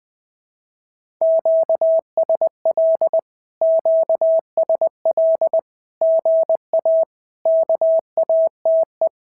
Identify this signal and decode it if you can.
Morse code, heard as a keyed tone, about 20 wpm, 650 Hz. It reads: QSL QSL GA KATE
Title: SOTA QSO Practice 20wpm